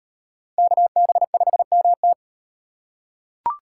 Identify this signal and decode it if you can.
Morse code, heard as a keyed tone, about 37 wpm, 700 Hz. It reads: KB5MT